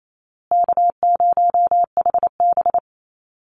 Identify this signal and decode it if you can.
Morse code, heard as a keyed tone, about 28 wpm, 700 Hz. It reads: K0HB
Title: Call Signs Repeated 2x - 28wpm